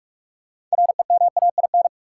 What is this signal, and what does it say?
Morse code, heard as a keyed tone, about 46 wpm, 700 Hz. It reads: REMAIN
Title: Sentences from Top 1200 Words - New Word Drill - 45wpm